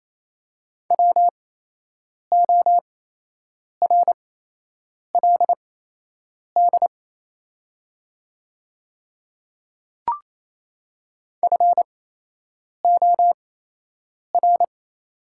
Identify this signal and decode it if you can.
Morse code, heard as a keyed tone, about 28 wpm, spaced as for 8 wpm, 700 Hz. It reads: WORLD FOR
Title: 8x Character Spacing - Top 300 Words - 28wpm